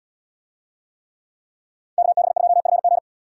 Morse code, heard as a keyed tone, about 76 wpm, 700 Hz. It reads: KB2FG